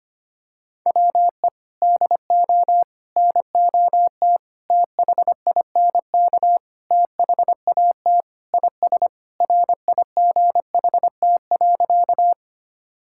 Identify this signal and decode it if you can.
Morse code, heard as a keyed tone, about 25 wpm, 700 Hz. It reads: WE DO NOT THINK THAT IS RIGHT.
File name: practice / 25wpm / Sentences with 4-Word Phrases - 25wpm